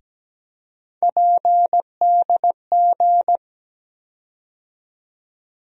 Morse code, heard as a keyed tone, about 17 wpm, 700 Hz. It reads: PDG
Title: USA Call Sign Postfixes - 3 Letters 17wpm